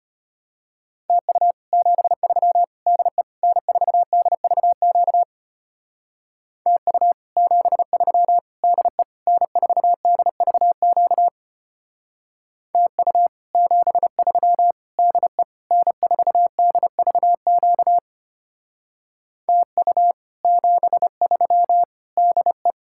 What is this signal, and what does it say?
Morse code, heard as a keyed tone, about 30 wpm, 700 Hz. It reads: TU 73 DE N4DVQ TU 73 DE N4DVQ TU 73 DE N4DVQ TU 73 DE
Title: Speed-Racing Common QSO Phrases - 25wpm